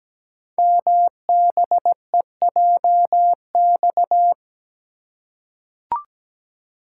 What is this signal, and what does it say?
Morse code, heard as a keyed tone, about 17 wpm, 700 Hz. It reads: MBEJX
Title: Code Groups - 5 Letters - 17wpm